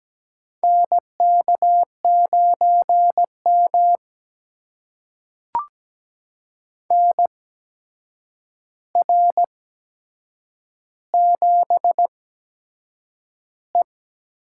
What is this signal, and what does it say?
Morse code, heard as a keyed tone, about 17 wpm, spaced as for 7 wpm, 700 Hz. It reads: NK9M NR7E
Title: USA Call Signs - 8x Character Spacing - 17wpm